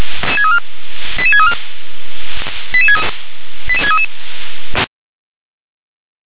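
Five-tone selective call (ZVEI), heard as quick runs of five tones, tones of about 70 ms. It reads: 88634 88633
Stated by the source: Received on an SDR.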